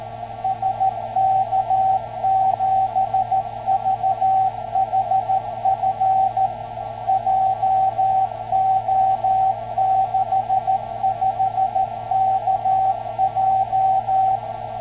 Morse code, heard as a keyed tone, about 14 wpm, 760 Hz, with radio noise. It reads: SK7VHF JO65KJ
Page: 144 MHz FT-847 + MGF1302 in shack 0,35dB NF + 950W Solid State PA + 1x 10el DK7ZB 3WL